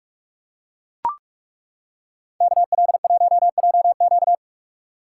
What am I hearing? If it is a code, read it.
Morse code, heard as a keyed tone, about 37 wpm, 700 Hz. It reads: KL1JQ
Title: Speed Racing - USA 2x2 Call Signs 30wpm